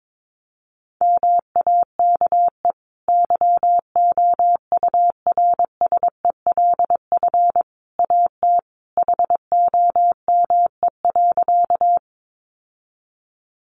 Morse code, heard as a keyed tone, about 22 wpm, 700 Hz. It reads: MAKE YOURSELF AT HOME.
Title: Sentences from Top 600 Words 22wpm